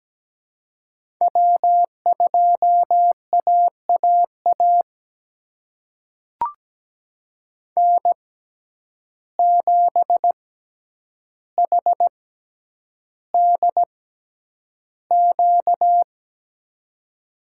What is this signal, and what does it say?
Morse code, heard as a keyed tone, about 17 wpm, spaced as for 9 wpm, 700 Hz. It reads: W2AAA N7HDQ
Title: USA Call Signs - 5 Characters - 6x Character Spacing - 17wpm